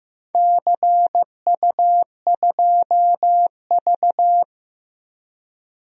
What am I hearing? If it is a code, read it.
Morse code, heard as a keyed tone, about 15 wpm, 700 Hz. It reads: CU2V
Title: DXCC-100 Call Signs - 4 Characters - 15wpm